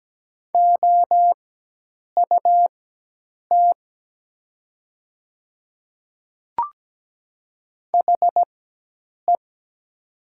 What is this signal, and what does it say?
Morse code, heard as a keyed tone, about 17 wpm, spaced as for 8 wpm, 700 Hz. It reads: OUT HE
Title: 4x Character Spacing - Top 100 Words - 17wpm